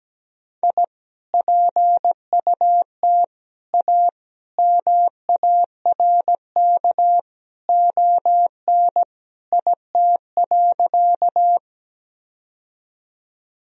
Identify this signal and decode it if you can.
Morse code, heard as a keyed tone, about 17 wpm, 700 Hz. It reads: I PUT A MARK ON IT.
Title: Sentences from Top 1000 Words - Review Part 1 - 17wpm